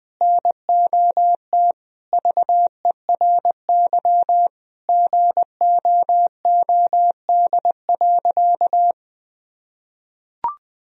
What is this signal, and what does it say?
Morse code, heard as a keyed tone, about 20 wpm, 700 Hz. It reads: NOT VERY GOOD.
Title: Sentences from Top 1000 Words - Review Part 3 - 20wpm